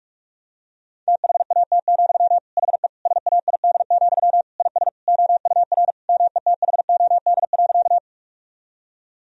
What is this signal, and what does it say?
Morse code, heard as a keyed tone, about 46 wpm, 700 Hz. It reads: THAT, HE SAID, IS OUR METHOD.
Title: Sentences from Top 1000 Words - Review Part 3 - 45wpm